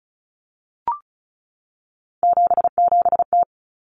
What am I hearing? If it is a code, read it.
Morse code, heard as a keyed tone, about 34 wpm, 700 Hz. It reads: 77T
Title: Common Cut Numbers - 3-Digit 35wpm